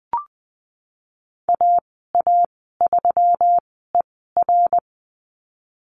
Morse code, heard as a keyed tone, about 20 wpm, spaced as for 15 wpm, 700 Hz. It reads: AA3ER
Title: 179 Lesson 29g - Slash - Call Signs 20wpm